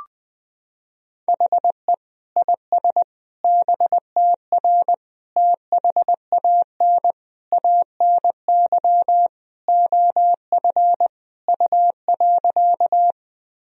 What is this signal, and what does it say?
Morse code, heard as a keyed tone, about 20 wpm, 700 Hz. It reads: HE IS BTR THAN ANY OF U.
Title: Sentences from Top 400 Words and Abbreviations 20wpm